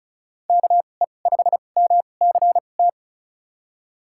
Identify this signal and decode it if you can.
Morse code, heard as a keyed tone, about 36 wpm, spaced as for 25 wpm, 700 Hz. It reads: KE5MCT
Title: USA Call Signs - 2x Character Spacing - 35wpm